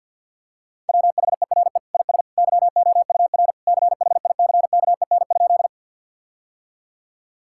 Morse code, heard as a keyed tone, about 52 wpm, 700 Hz. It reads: WHERE IS YOUR CHICKEN?
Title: Sentences from Top 1000 Words - Review Part 5 - 50wpm